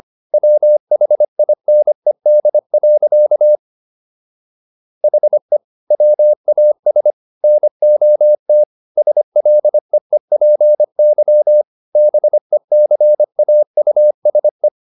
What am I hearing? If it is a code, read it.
Morse code, heard as a keyed tone, about 25 wpm, 600 Hz. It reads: WHINED. HE WAS NOT SLEEPY BECAUSE